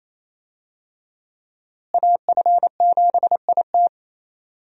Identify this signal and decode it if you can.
Morse code, heard as a keyed tone, about 28 wpm, 700 Hz. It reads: AF7IT
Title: USA Call Signs - 5 Characters - Repeated 3x - 28wpm